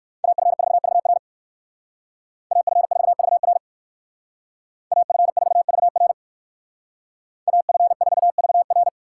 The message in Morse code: AF4VR AF4VR AF4VR AF4VR

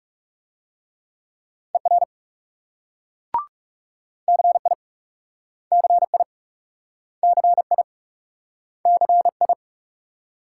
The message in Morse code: ER CI CI CI CI

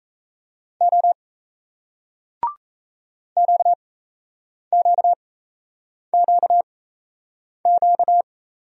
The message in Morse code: O QQQQ

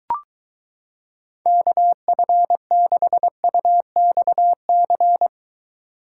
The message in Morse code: KF6UXC

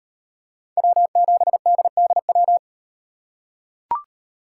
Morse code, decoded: W7DDW

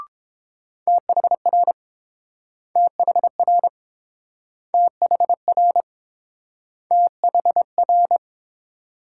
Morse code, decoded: THR THR THR THR

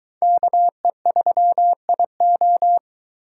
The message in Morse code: KE3IO

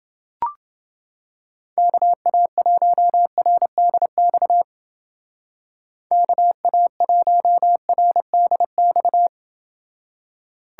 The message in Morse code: KA1RDX KA1RDX

